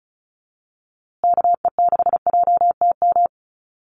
Morse code, heard as a keyed tone, about 34 wpm, 700 Hz. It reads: KE6JTM
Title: USA Call Signs Repeated 3x 35wpm